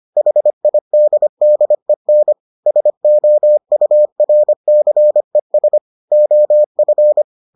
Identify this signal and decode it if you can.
Morse code, heard as a keyed tone, about 25 wpm, 600 Hz. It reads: HIDDEN SOURCES OF